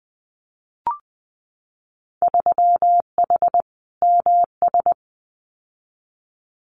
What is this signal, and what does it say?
Morse code, heard as a keyed tone, about 20 wpm, 700 Hz. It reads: 3H MS
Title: Winter Field Day Exchanges Encore 20wpm